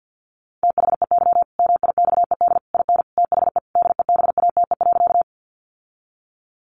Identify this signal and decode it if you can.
Morse code, heard as a keyed tone, about 50 wpm, 700 Hz. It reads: THEY MIXED IN THE DEBATE.
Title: Sentences from Top 1500 Words 50wpm